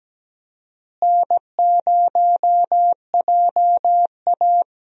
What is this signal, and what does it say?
Morse code, heard as a keyed tone, about 17 wpm, 700 Hz. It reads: N0JA
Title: USA Call Signs - 4 Characters - Repeated 2x - 17wpm